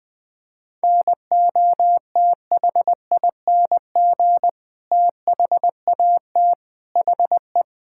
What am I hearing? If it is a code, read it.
Morse code, heard as a keyed tone, about 20 wpm, 700 Hz. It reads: NOTHING THAT HE
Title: Sentences from Top 1000 Words - Review Part 6 - 20wpm